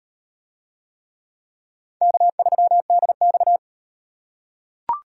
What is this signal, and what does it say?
Morse code, heard as a keyed tone, about 38 wpm, 700 Hz. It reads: K3DX